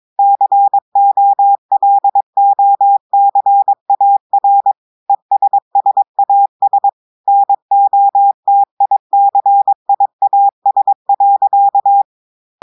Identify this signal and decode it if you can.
Morse code, heard as a keyed tone, about 22 wpm, 800 Hz. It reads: COLOCAR ESSAS NOTICIAS.
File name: audios / portugais / portugais texte en clair 2 vitesse 22 mots minutes